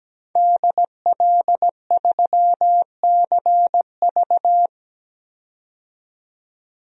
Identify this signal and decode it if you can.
Morse code, heard as a keyed tone, about 17 wpm, 700 Hz. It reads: DL3CV
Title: DXCC-100 Call Signs - 5 Characters - 17wpm